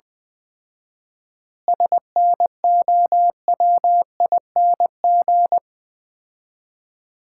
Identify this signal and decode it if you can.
Morse code, heard as a keyed tone, about 21 wpm, 700 Hz. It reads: SNOWING
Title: Common QSO Phrases - 4x Word-Spacing - 20wpm